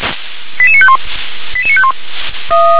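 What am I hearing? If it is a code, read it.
Five-tone selective call (ZVEI), heard as quick runs of five tones, tones of about 70 ms. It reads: 88951 88951
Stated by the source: ZVEI five-tone selective call received by SDR.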